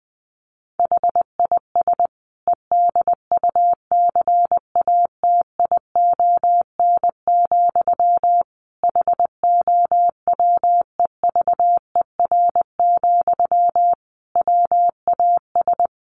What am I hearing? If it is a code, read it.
Morse code, heard as a keyed tone, about 20 wpm, 700 Hz. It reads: HIS EDUCATION, HOWEVER, WAS
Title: Sentences from Top 1400 Words 20wpm